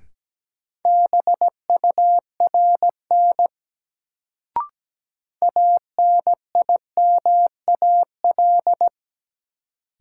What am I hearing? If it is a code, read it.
Morse code, heard as a keyed tone, about 17 wpm, 700 Hz. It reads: BURN ANIMAL
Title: Sentences from Top 1300 Words - New Word Drill Alt - 17wpm